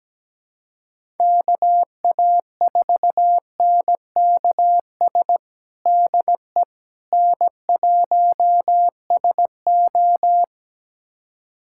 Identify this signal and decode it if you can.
Morse code, heard as a keyed tone, about 17 wpm, 700 Hz. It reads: KA4NKS DE N1SO
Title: USA Callsign DE Callsign 17wpm